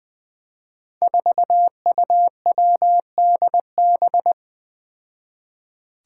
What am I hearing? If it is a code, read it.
Morse code, heard as a keyed tone, about 20 wpm, 700 Hz. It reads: 4UWDB
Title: Five Challenging Letter-Number Combinations 20wpm